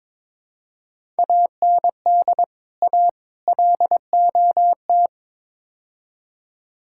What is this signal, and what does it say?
Morse code, heard as a keyed tone, about 21 wpm, 700 Hz. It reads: AND A LOT
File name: practice / 22wpm / Build A Phrase - 5-Word Phrases - 22wpm